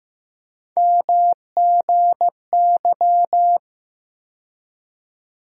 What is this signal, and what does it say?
Morse code, heard as a keyed tone, about 15 wpm, 700 Hz. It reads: MGY
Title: USA Call Sign Postfixes - 3 Letters 15wpm